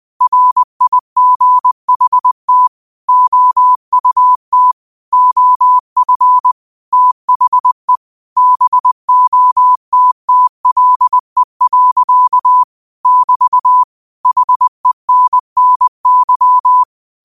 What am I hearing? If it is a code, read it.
Morse code, heard as a keyed tone, about 20 wpm, 1000 Hz. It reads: RIGHT OUT OF THE BOTTLE. <BT> HENNY
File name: qotdfiles / 20wpm